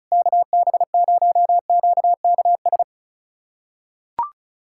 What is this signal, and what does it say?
Morse code, heard as a keyed tone, about 34 wpm, 700 Hz. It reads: KB0QKS
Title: USA Callsign DE Callsign 35wpm